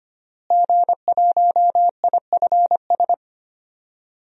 Morse code, heard as a keyed tone, about 25 wpm, 700 Hz. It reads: G1IFS
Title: DXCC-100 Call Signs - 5 Characters - Repeated 3x - 25wpm